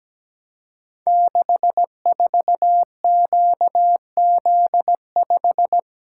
Morse code, heard as a keyed tone, about 17 wpm, 700 Hz. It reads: 64QZ5
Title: Code Groups - 5 Letter:Numbers - 17wpm